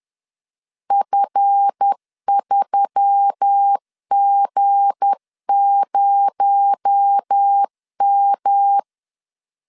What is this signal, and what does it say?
Morse code, heard as a keyed tone, about 10 wpm, 800 Hz. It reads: F3G0M
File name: CW / DET / 1 9 1500C S A 10wpm 4511